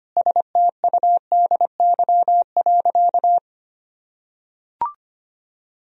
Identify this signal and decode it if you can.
Morse code, heard as a keyed tone, about 25 wpm, 700 Hz. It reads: STUDY.
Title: Sentences from Top 1000 Words - Review Part 1 - 25wpm